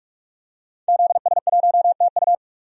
Morse code, heard as a keyed tone, about 42 wpm, 700 Hz. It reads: ZS1TU
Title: DXCC-100 Call Signs - 5 Characters - 45wpm